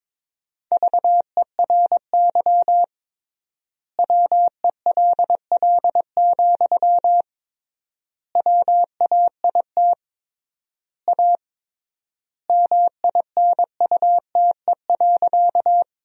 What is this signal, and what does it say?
Morse code, heard as a keyed tone, about 22 wpm, 700 Hz. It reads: VERY WELL, WAIT A MINUTE.
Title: 3x Word-Spacing - Sentences from Top 500 Words - 22wpm